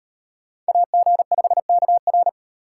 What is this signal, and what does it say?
Morse code, heard as a keyed tone, about 36 wpm, 700 Hz. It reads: AG5KR